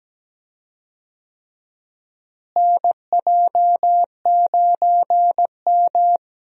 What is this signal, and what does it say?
Morse code, heard as a keyed tone, about 17 wpm, 700 Hz. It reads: NJ9M
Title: USA Call Signs - 4 Characters - Repeated 3x - 17wpm